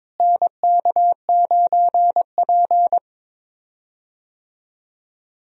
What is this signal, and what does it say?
Morse code, heard as a keyed tone, about 23 wpm, 700 Hz. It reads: NK9P
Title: USA Call Signs - 4 Characters - Repeated 2x - 22wpm